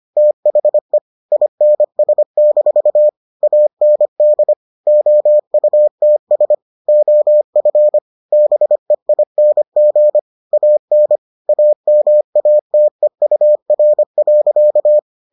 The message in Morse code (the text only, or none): THE INS- AND OUTS OF BEING AN AMATEUR.